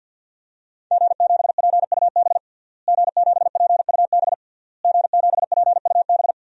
Morse code, heard as a keyed tone, about 53 wpm, 700 Hz. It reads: G7PUB G7PUB G7PUB
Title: DXCC-100 Call Signs - 5 Characters - Repeated 3x - 50wpm